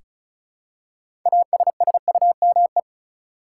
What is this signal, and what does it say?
Morse code, heard as a keyed tone, about 36 wpm, 700 Hz. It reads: ASSUME
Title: Six Letter Words - 35wpm